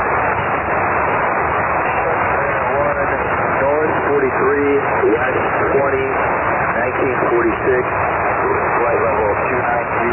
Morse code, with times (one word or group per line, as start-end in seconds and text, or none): none